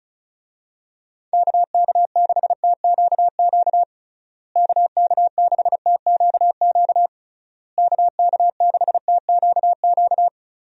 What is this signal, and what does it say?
Morse code, heard as a keyed tone, about 34 wpm, 700 Hz. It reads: KK6TQQ KK6TQQ KK6TQQ
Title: USA Call Signs - 6 Characters - Repeated 3x - 35wpm